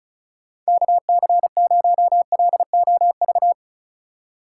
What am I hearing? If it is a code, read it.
Morse code, heard as a keyed tone, about 36 wpm, 700 Hz. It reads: KC0LOV
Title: USA Call Signs - 6 Characters - 4x Character Spacing - 35wpm